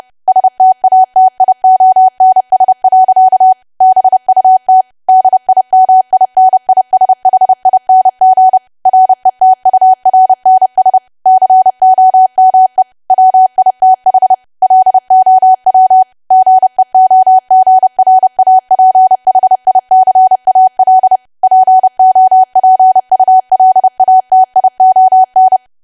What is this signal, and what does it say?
Morse code, heard as a keyed tone, about 30 wpm, 750 Hz. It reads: STATIONS. BUT DIMINISHING RETURNS COME WITH LOW GEOGRAPHICAL POPULATION
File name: cwa / W1AW-Level3 / 30WPM